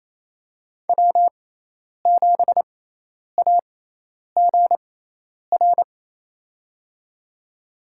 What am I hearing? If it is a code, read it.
Morse code, heard as a keyed tone, about 29 wpm, spaced as for 10 wpm, 700 Hz. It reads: W7AGR
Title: USA Call Signs - 5 Characters - 6x Character Spacing - 28wpm